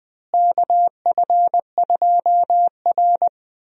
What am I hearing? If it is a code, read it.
Morse code, heard as a keyed tone, about 20 wpm, 700 Hz. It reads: KF2R
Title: USA Call Signs - 4 Characters - Repeated 2x - 20wpm